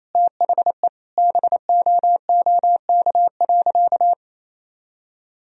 Morse code, heard as a keyed tone, about 28 wpm, 700 Hz. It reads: THE BOOK.
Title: Sentences from Top 1000 Words - Review Part 1 - 28wpm